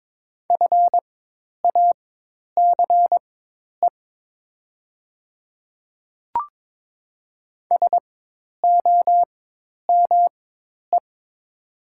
Morse code, heard as a keyed tone, about 22 wpm, spaced as for 10 wpm, 700 Hz. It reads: FACE SOME
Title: 4x Character Spacing - Top 500 Words - 22wpm